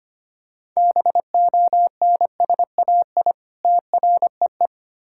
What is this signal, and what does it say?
Morse code, heard as a keyed tone, about 25 wpm, 700 Hz. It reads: BONSAI TREE
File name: practice / 25wpm / Difficult Words with Hints - 25wpm